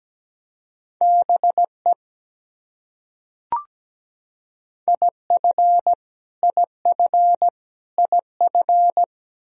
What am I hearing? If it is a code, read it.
Morse code, heard as a keyed tone, about 17 wpm, 700 Hz. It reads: BE IF IF IF
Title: Top 100 Words Repeated 17wpm